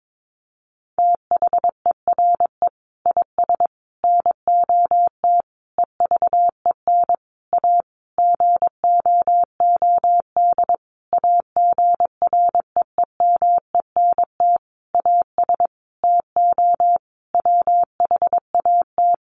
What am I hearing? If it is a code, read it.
Morse code, heard as a keyed tone, about 22 wpm, 700 Hz. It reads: THERE IS NOT EVEN A GOOD AGREEMENT AS TO WHAT
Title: Sentences from Top 1500 Words 22wpm